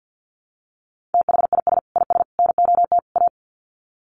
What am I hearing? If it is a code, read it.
Morse code, heard as a keyed tone, about 48 wpm, 700 Hz. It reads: THIS IS NOT A
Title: Sets of 4 Words 50wpm